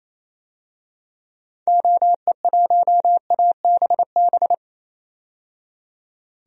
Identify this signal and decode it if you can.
Morse code, heard as a keyed tone, about 28 wpm, 700 Hz. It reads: OE1ABB
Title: DXCC-100 Call Signs - 6 Characters - 28wpm